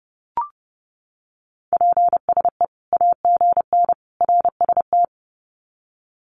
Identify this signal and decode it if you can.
Morse code, heard as a keyed tone, about 30 wpm, 700 Hz. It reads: PSE AGN RST
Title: Common QSO Phrases 30wpm